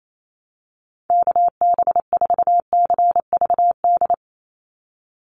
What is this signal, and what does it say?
Morse code, heard as a keyed tone, about 28 wpm, 700 Hz. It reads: KB4CVD
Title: USA Call Signs 28wpm